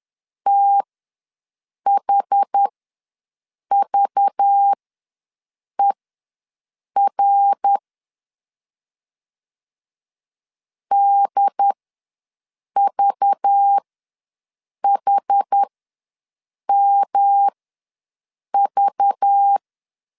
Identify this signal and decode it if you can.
Morse code, heard as a keyed tone, about 10 wpm, spaced as for 6 wpm, 800 Hz. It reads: THVER DVHMV
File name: LEZIONI / 5 LBVC / 5 LBV ws 10wpm 3645